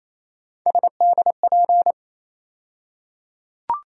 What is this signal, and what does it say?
Morse code, heard as a keyed tone, about 28 wpm, 700 Hz. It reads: SDP
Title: USA Call Sign Postfixes - 3 Letters 28wpm